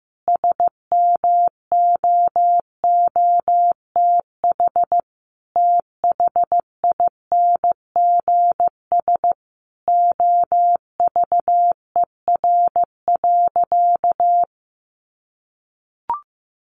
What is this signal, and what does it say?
Morse code, heard as a keyed tone, about 15 wpm, 700 Hz. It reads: SMOOTH THINGS OVER.
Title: Sentences from Top 1400 Words 15wpm